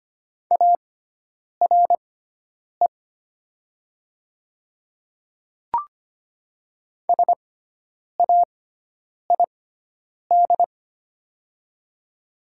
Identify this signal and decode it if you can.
Morse code, heard as a keyed tone, about 25 wpm, spaced as for 9 wpm, 700 Hz. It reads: ARE SAID
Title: 6x Character Spacing - Top 100 Words - 25wpm